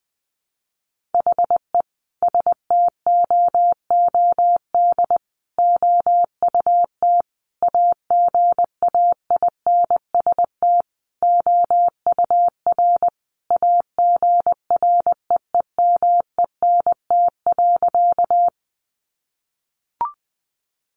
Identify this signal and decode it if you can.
Morse code, heard as a keyed tone, about 20 wpm, 700 Hz. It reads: HE STOOD OUT AGAINST OUR AGREEMENT.
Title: Sentences from Top 1500 Words 20wpm